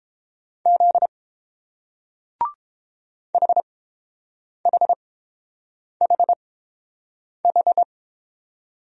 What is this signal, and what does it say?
Morse code, heard as a keyed tone, about 30 wpm, spaced as for 7 wpm, 690 Hz. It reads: Z HHHH